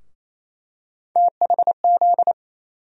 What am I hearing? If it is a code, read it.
Morse code, heard as a keyed tone, about 28 wpm, 700 Hz. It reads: THZ
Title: USA Call Sign Postfixes - 3 Letters 28wpm